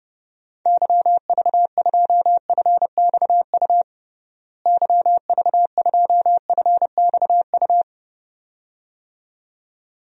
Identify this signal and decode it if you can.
Morse code, heard as a keyed tone, about 32 wpm, 700 Hz. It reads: YV2FXU YV2FXU
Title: DXCC-100 Call Signs - 6 Characters - Repeated 2x - 30wpm